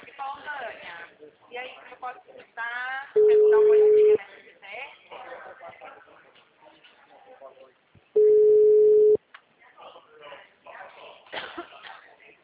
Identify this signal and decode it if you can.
Ringback tone (Europe, 425 Hz), with speech.